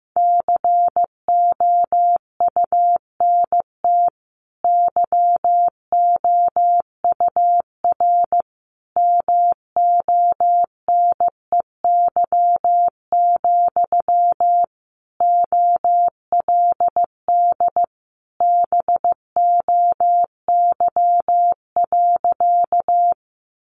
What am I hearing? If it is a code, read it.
Morse code, heard as a keyed tone, about 15 wpm, 700 Hz. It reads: COUNT YOUR MONEY, OLD BOY.
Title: Sentences from Top 1500 Words 15wpm